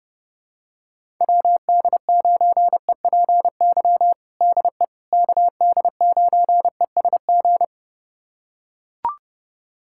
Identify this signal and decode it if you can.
Morse code, heard as a keyed tone, about 30 wpm, 700 Hz. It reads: WD9EPY DE KD9ESG
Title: USA Callsign DE Callsign 30wpm